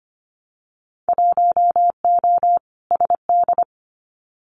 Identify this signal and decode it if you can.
Morse code, heard as a keyed tone, about 25 wpm, 700 Hz. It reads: 1O SD
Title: Winter Field Day Exchanges Encore 25wpm